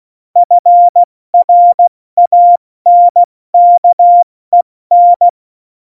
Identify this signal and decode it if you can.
Morse code, heard as a keyed tone, about 16 wpm, spaced as for 14 wpm, 700 Hz. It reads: FRANKEN
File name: Morsekurs / Witze / Witz 15. Oktober 2019, 70 F80